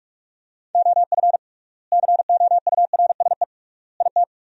Morse code, heard as a keyed tone, about 41 wpm, 700 Hz. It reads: OF COURSE IT
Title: 3x Word-Spacing - Sentences from Top 500 Words - 45wpm